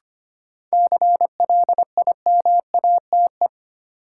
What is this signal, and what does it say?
Morse code, heard as a keyed tone, about 25 wpm, 700 Hz. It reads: CLIMATE
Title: Sentences from Top 1600 Words - New Word Drill Alt - 25wpm